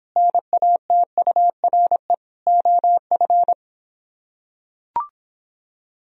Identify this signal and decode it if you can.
Morse code, heard as a keyed tone, about 26 wpm, 700 Hz. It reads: NATURE OF